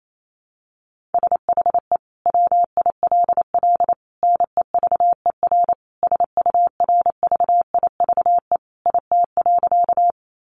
Morse code, heard as a keyed tone, about 28 wpm, 700 Hz. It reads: SHE WILL NEVER SURVIVE IT.
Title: Sentences from Top 1300 Words 28wpm